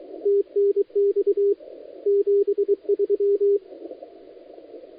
Morse code, heard as a keyed tone, about 23 wpm, 400 Hz, with radio noise. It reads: TNX 73
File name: dlazaawansowanych / nagraniazpasma / titawaNETcontrol